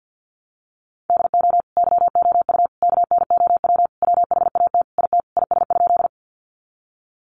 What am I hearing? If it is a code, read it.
Morse code, heard as a keyed tone, about 50 wpm, 700 Hz. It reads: DO YOU KNOW WHAT IT IS?
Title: Sentences from Top 100 Words 50wpm